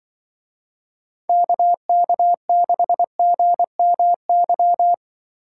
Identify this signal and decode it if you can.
Morse code, heard as a keyed tone, about 23 wpm, 700 Hz. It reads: KK6GMY
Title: Speed Racing - USA 2x3 Call Signs 20wpm